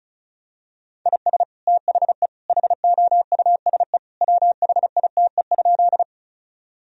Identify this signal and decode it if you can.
Morse code, heard as a keyed tone, about 36 wpm, 700 Hz. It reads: IS THE HOUSE WHITE?
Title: Sentences from Top 1000 Words - Review Part 1 - 35wpm